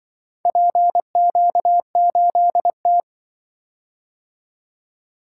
Morse code, heard as a keyed tone, about 24 wpm, 700 Hz. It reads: PQ8T
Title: Speed-Racing - Four Letter-Number Combinations - 20wpm